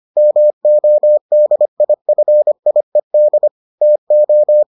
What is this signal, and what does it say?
Morse code, heard as a keyed tone, about 25 wpm, 600 Hz. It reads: MODIFIED TO